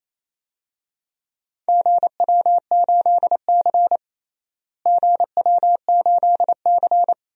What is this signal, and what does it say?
Morse code, heard as a keyed tone, about 28 wpm, 700 Hz. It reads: GW8C GW8C
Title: DXCC-100 Call Signs - 4 Characters - Repeated 2x - 28wpm